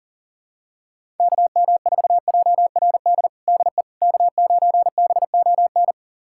Morse code, heard as a keyed tone, about 37 wpm, 700 Hz. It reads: KM4JRD DE K9BON
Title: USA Callsign DE Callsign 40wpm